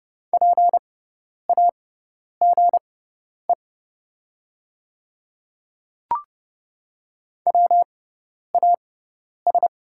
Morse code, heard as a keyed tone, about 30 wpm, spaced as for 10 wpm, 700 Hz. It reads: PAGE WAS
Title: 6x Character Spacing - Top 300 Words - 30wpm